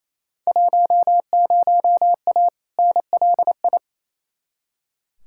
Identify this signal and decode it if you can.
Morse code, heard as a keyed tone, about 28 wpm, 700 Hz. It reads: 10A NLI
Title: Field Day Exchanges Encore - 28wpm